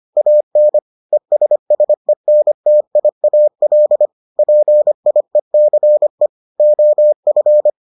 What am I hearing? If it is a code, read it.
Morse code, heard as a keyed tone, about 25 wpm, 600 Hz. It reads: AN ESSENTIAL PIECE OF